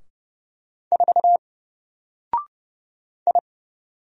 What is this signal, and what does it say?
Morse code, heard as a keyed tone, about 31 wpm, 700 Hz. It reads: 4 I
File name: Single-Letter-Number-30wpm